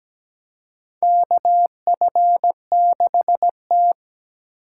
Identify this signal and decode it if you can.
Morse code, heard as a keyed tone, about 17 wpm, 700 Hz. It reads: KF6T
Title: K1USN SST QSO Practice - 17wpm